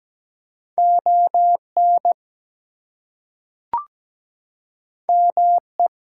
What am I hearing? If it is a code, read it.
Morse code, heard as a keyed tone, about 17 wpm, 700 Hz. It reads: ON ME
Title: Two Letter Words - 17wpm